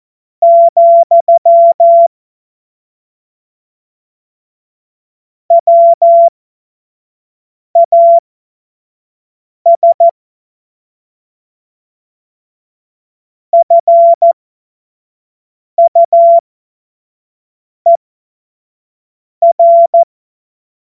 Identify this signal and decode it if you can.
Morse code, heard as a keyed tone, about 14 wpm, spaced as for 5 wpm, 680 Hz. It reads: , WAS FUER
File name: Morsekurs / Witze / Witz 26. Oktober 2021, 25 F70